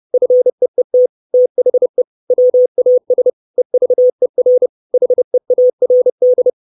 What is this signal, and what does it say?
Morse code, heard as a keyed tone, about 30 wpm, 500 Hz. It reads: FEET THE WAS EVER HEARD
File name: 632 Words 30WPM